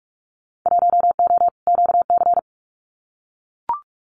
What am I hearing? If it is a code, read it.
Morse code, heard as a keyed tone, about 44 wpm, 700 Hz. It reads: 1O QC
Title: Winter Field Day Exchanges 45wpm